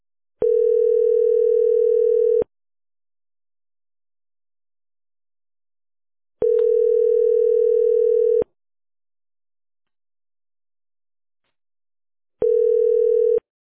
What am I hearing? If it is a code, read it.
Ringback tone (North America), 440 + 480 Hz.